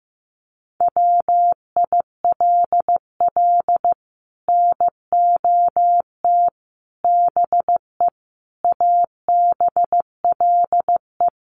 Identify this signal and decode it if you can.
Morse code, heard as a keyed tone, about 15 wpm, 700 Hz. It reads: WILL NOT BE ABLE
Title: Sets of 5 Words 15wpm